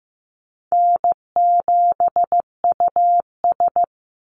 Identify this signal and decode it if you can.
Morse code, heard as a keyed tone, about 15 wpm, 700 Hz. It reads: N7US
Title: Call Signs Repeated 2x - 15wpm